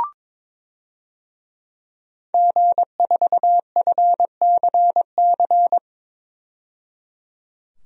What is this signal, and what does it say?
Morse code, heard as a keyed tone, about 21 wpm, 700 Hz. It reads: G4FCC
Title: DXCC-100 Call Signs - 5 Characters - 22wpm